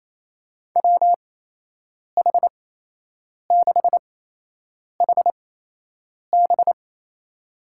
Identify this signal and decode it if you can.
Morse code, heard as a keyed tone, about 27 wpm, spaced as for 8 wpm, 700 Hz. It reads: WH6HB
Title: USA Call Signs - 5 Characters - 8x Character Spacing - 28wpm